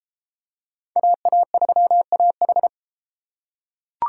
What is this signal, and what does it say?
Morse code, heard as a keyed tone, about 33 wpm, 700 Hz. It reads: AA3AH